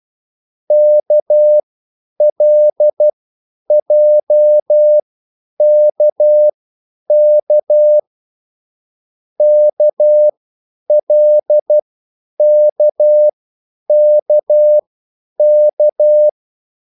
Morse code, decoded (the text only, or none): KLJKK KLKKK